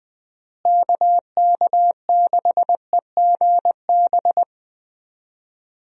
Morse code, heard as a keyed tone, about 20 wpm, 700 Hz. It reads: KK6EGB
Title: USA Call Signs - 6 Characters - Repeated 3x - 20wpm